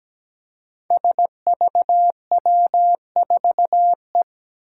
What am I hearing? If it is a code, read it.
Morse code, heard as a keyed tone, about 17 wpm, 700 Hz. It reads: SVW4E
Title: Five Challenging Letter-Number Combinations 17wpm